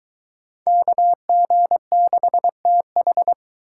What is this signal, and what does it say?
Morse code, heard as a keyed tone, about 22 wpm, 700 Hz. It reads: KG6TH